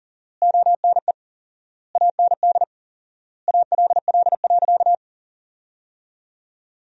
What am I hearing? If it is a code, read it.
Morse code, heard as a keyed tone, about 40 wpm, 700 Hz. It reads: ONE AND ALL.
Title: 4x Word-Spacing - Sentences from Top 200 Words - 40wpm